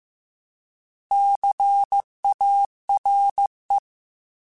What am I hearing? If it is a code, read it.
Morse code, heard as a keyed tone, about 15 wpm, 800 Hz. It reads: CARE
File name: Alphabetical / Alphabetical mp3s / Cs @13